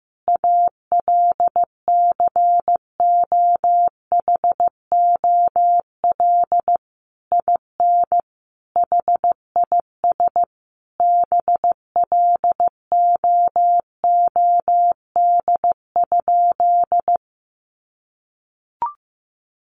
Morse code, heard as a keyed tone, about 15 wpm, 700 Hz. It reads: ALCOHOL IN HIS BLOOD?
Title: Sentences from Top 1400 Words 15wpm